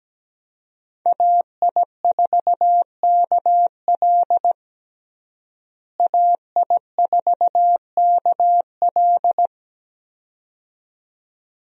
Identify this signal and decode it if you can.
Morse code, heard as a keyed tone, about 17 wpm, 700 Hz. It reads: AI4KL AI4KL
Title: USA Call Signs - 5 Characters - Repeated 2x - 17wpm